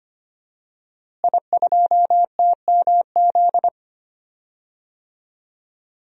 Morse code, heard as a keyed tone, about 25 wpm, 700 Hz. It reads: I2TMZ
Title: DXCC-100 Call Signs - 5 Characters - Repeated 3x - 25wpm